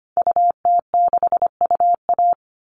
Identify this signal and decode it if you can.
Morse code, heard as a keyed tone, about 25 wpm, 700 Hz. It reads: UT6UA
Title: Call Signs Repeated 2x - 25wpm